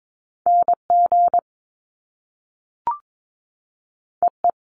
Morse code, heard as a keyed tone, about 22 wpm, 700 Hz. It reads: NG EE
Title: Top Two Letter Combinations in Words 22wpm